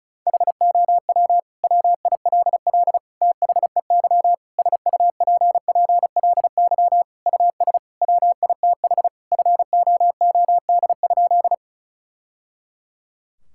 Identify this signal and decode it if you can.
Morse code, heard as a keyed tone, about 36 wpm, 690 Hz. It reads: HOW WILL THEY SUPPLY US WITH FOOD?
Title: Sentences from Top 1000 Words - Review Part 5 - 35wpm